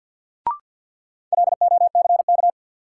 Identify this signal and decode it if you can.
Morse code, heard as a keyed tone, about 52 wpm, 700 Hz. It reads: LOCK
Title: Sentences from Top 1300 Words - New Word Drill Alt - 50wpm